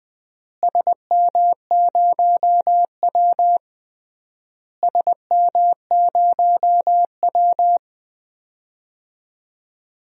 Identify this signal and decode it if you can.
Morse code, heard as a keyed tone, about 20 wpm, 700 Hz. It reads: SM0W SM0W
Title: DXCC-100 Call Signs - 4 Characters - Repeated 3x - 20wpm